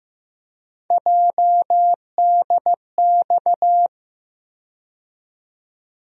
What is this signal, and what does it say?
Morse code, heard as a keyed tone, about 15 wpm, 700 Hz. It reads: JDX
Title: USA Call Sign Postfixes - 3 Letters 15wpm